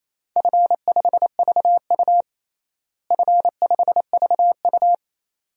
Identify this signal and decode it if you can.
Morse code, heard as a keyed tone, about 28 wpm, 700 Hz. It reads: F5VU F5VU
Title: DXCC-100 Call Signs - 4 Characters - Repeated 2x - 28wpm